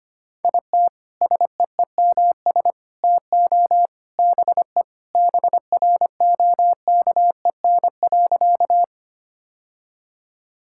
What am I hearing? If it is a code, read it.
Morse code, heard as a keyed tone, about 25 wpm, 700 Hz. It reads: IT SEEMS TO BE BROKEN.
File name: practice / 25wpm / Sentences with 3-Word Phrases - 25wpm